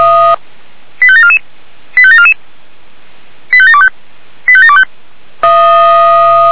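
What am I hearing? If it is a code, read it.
Five-tone selective call (ZVEI), heard as quick runs of five tones, tones of about 80 ms. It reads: 85744 85744 85725 85725
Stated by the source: ZVEI five-tone selective call received via SDR.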